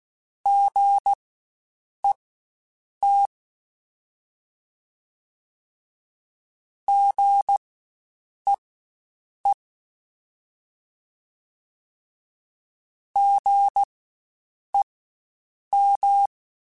Morse code, heard as a keyed tone, about 16 wpm, spaced as for 7 wpm, 800 Hz. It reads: GET GEE GEM
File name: Spaced Alphas / Spaced Alphas mp3s / Gs sp @15